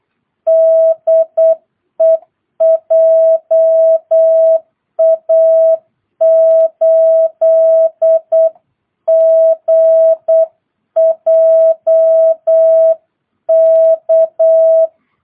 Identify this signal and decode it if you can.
Morse code, heard as a keyed tone, about 8 wpm, 660 Hz. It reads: DE JA8GJK
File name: jh8cbh / 2022cw